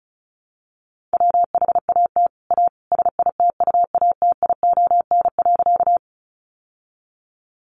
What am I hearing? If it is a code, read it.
Morse code, heard as a keyed tone, about 34 wpm, 700 Hz. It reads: WHAT A SITUATION.
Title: Sentences from Top 1200 Words 35wpm